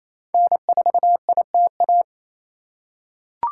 Morse code, heard as a keyed tone, about 28 wpm, 700 Hz. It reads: N4ITA
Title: USA Call Signs - 5 Characters - 2x Character Spacing - 28wpm